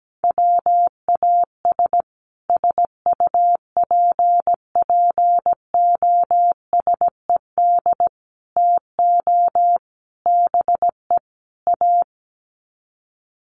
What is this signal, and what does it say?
Morse code, heard as a keyed tone, about 17 wpm, 700 Hz. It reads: WAS SUPPOSED TO BE A
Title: Sets of 5 Words Encore 17wpm